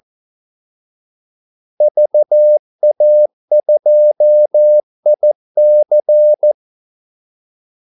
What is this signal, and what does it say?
Morse code, heard as a keyed tone, about 14 wpm, 600 Hz. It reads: VA2IC
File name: Callsigns-14wpm